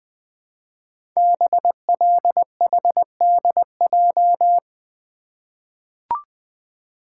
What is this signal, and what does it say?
Morse code, heard as a keyed tone, about 20 wpm, 700 Hz. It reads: BLHDJ
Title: Code Groups - 5 Letters - 20wpm